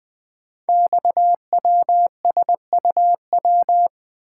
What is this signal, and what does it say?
Morse code, heard as a keyed tone, about 20 wpm, 700 Hz. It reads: XWSUW
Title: Code Groups - 5 Letters - 20wpm